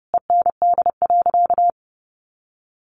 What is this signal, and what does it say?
Morse code, heard as a keyed tone, about 28 wpm, 700 Hz. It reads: END.
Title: Sentences from Top 1300 Words 30wpm